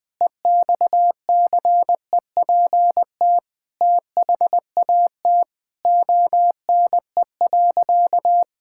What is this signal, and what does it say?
Morse code, heard as a keyed tone, about 20 wpm, 700 Hz. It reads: EXCEPT THAT ONE.
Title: Sentences from Top 1600 Words 20wpm